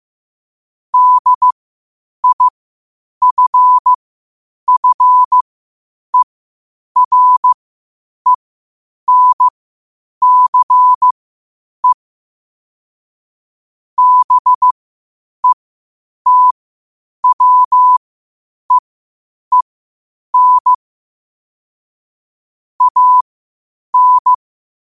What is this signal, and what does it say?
Morse code, heard as a keyed tone, about 15 wpm, spaced as for 8 wpm, 1000 Hz. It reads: DIFFERENCE BETWEEN AN
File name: 8wpm